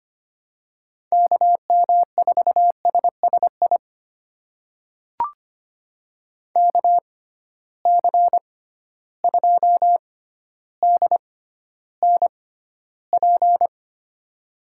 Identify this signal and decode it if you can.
Morse code, heard as a keyed tone, about 25 wpm, spaced as for 13 wpm, 700 Hz. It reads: KM4SSI KC2DNP
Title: USA Call Signs - 6 Characters - 6x Character Spacing - 25wpm